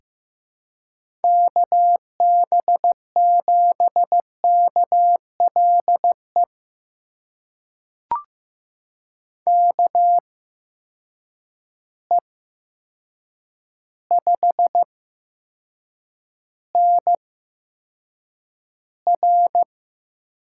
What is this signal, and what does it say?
Morse code, heard as a keyed tone, about 15 wpm, spaced as for 7 wpm, 700 Hz. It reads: KB7KLE KE5NR
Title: USA Call Signs - 6 Characters - 8x Character Spacing - 15wpm